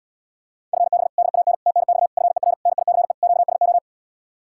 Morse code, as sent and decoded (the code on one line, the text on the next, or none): ..-. -... / -.. .- -. / - -. -..- / ..-. .-. / -. .. -.-. / --.- ... ---
FB DAN TNX FER NICE QSO